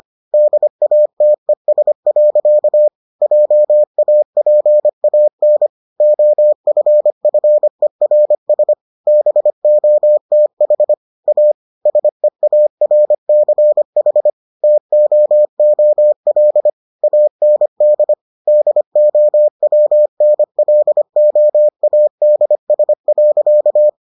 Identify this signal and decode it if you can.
Morse code, heard as a keyed tone, about 25 wpm, 600 Hz. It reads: DATES. JAPAN OFFERS BOTH A SEARCH TOOL AND DOWNLOADS.